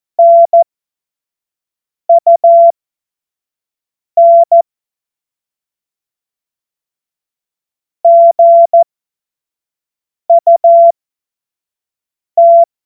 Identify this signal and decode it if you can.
Morse code, heard as a keyed tone, about 14 wpm, spaced as for 5 wpm, 680 Hz. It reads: NUN GUT
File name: Morsekurs / Witze / Witz 10. Mai 2022 25 F70